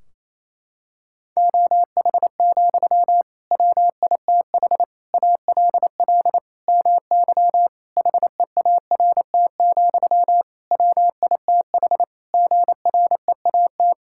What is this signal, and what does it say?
Morse code, heard as a keyed tone, about 28 wpm, 700 Hz. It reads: OH, WITH ALL MY HEART, WITH GREAT
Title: Sentences from Top 1600 Words 28wpm